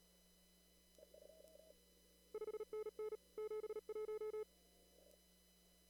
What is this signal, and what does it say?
Morse code, heard as a keyed tone, about 38 wpm, 430 Hz. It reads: 5NN 71
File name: CW / kt5j wpx cw-op-k5pi